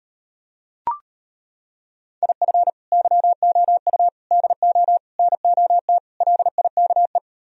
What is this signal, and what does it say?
Morse code, heard as a keyed tone, about 32 wpm, 700 Hz. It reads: IF YOU DO NOT LIKE